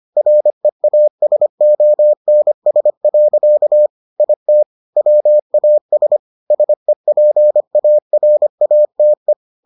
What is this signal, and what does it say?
Morse code, heard as a keyed tone, about 25 wpm, 600 Hz. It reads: REASONS. IT WAS SEPARATE